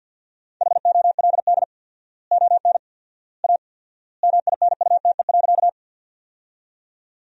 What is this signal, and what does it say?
Morse code, heard as a keyed tone, about 47 wpm, 700 Hz. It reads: HOLD ON A MINUTE.
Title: 4x Word-Spacing - Sentences from Top 500 Words - 50wpm